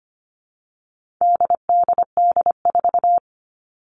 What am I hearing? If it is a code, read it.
Morse code, heard as a keyed tone, about 25 wpm, 700 Hz. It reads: DDD4
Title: All Cut Numbers - 4-Digit 25wpm